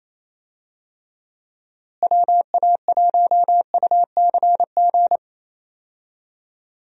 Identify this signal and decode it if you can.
Morse code, heard as a keyed tone, about 28 wpm, 700 Hz. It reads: WA1UCG
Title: USA Call Signs - 6 Characters - Repeated 2x - 28wpm